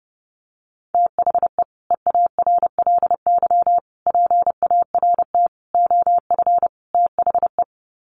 Morse code, heard as a keyed tone, about 30 wpm, 700 Hz. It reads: THE EARLY PART OF THE
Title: Sets of 5 Words Encore2 30wpm